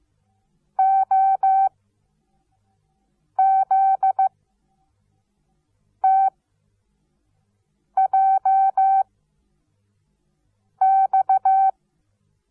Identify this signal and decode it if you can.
Morse code, heard as a keyed tone, about 15 wpm, 770 Hz. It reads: O Z T J X
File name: Onlinelehrgang / morsen / DJ4UF Morsen17